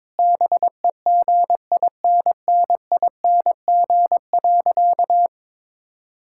Morse code, heard as a keyed tone, about 22 wpm, 700 Hz. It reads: BEGINNING.
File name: practice / 22wpm / Sentences with 3-Word Phrases - 22wpm